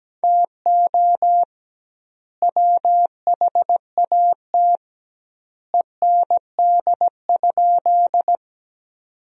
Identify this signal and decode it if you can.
Morse code, heard as a keyed tone, about 17 wpm, 700 Hz. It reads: TO WHAT END?
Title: 2x Word-Spacing - Sentences from Top 200 Words - 17wpm